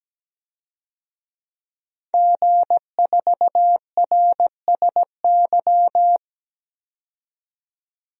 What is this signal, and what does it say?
Morse code, heard as a keyed tone, about 17 wpm, 700 Hz. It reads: G4RSY
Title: DXCC-100 Call Signs - 5 Characters - 17wpm